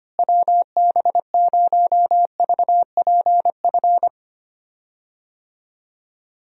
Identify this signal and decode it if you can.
Morse code, heard as a keyed tone, about 25 wpm, 700 Hz. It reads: WB0VPF
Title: USA Call Signs - 6 Characters - Repeated 3x - 25wpm